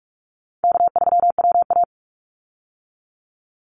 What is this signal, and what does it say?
Morse code, heard as a keyed tone, about 44 wpm, 700 Hz. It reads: K3WA
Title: Easy Call Signs Repeated 2x - 45wpm